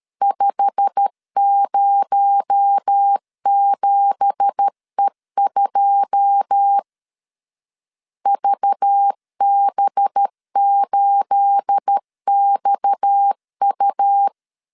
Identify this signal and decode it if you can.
Morse code, heard as a keyed tone, about 13 wpm, 800 Hz. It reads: 507E2 VB8XU
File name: CW / DET / 1 9 1500C S C 12wpm 3811